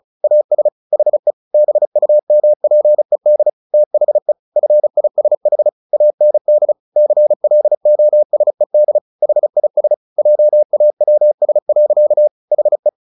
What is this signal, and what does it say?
Morse code, heard as a keyed tone, about 34 wpm, 600 Hz. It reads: AS HE BUMPED THE FISH AND CLOSED HIS JAWS. HE
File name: the old man and the sea 2 35wpm